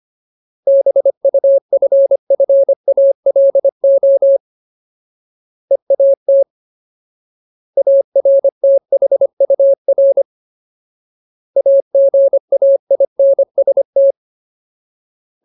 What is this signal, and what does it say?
Morse code, heard as a keyed tone, about 25 wpm, 550 Hz. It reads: BUFFALO EAT ARTHUR AGAINST
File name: Ultimate Word List / 25WPM Ultimate